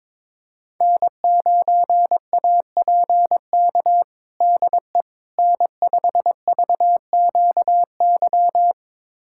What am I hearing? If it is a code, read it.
Morse code, heard as a keyed tone, about 22 wpm, 700 Hz. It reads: N9APK DE N5VQY
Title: USA Callsign DE Callsign - 5 Characters - 22wpm